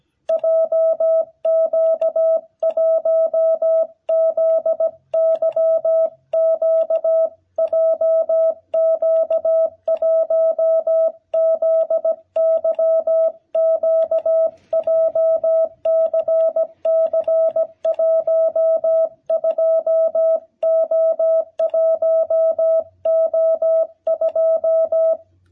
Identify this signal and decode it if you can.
Morse code, heard as a keyed tone, about 17 wpm, 660 Hz. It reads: JQ1ZYQ JQ1ZYQ JCC12O1O2
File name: jh8cbh / 2021cw